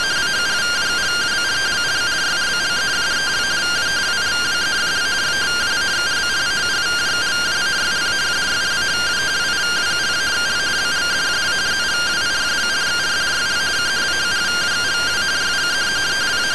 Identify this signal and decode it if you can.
Two-tone teleprinter signal, about 170 Hz shift.